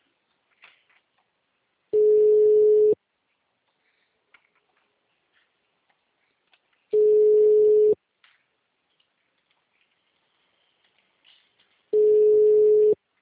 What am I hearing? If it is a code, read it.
Ringback tone (Europe, 425 Hz).
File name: out- -2024- - - .5040